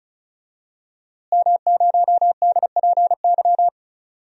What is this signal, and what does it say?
Morse code, heard as a keyed tone, about 36 wpm, 700 Hz. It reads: M0DPY
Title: DXCC-100 Call Signs - 5 Characters - Repeated 3x - 35wpm